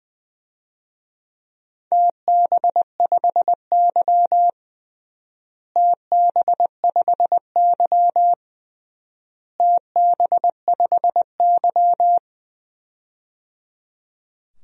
Morse code, heard as a keyed tone, about 20 wpm, 700 Hz. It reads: TB5Y TB5Y TB5Y
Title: DXCC-100 Call Signs - 4 Characters - Repeated 3x - 20wpm